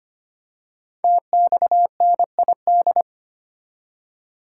Morse code, heard as a keyed tone, about 25 wpm, 700 Hz. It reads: TXNID
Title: Code Groups - 5 Letter:Numbers - 25wpm